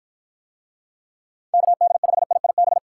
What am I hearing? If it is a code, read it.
Morse code, heard as a keyed tone, about 53 wpm, 700 Hz. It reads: KD5IIB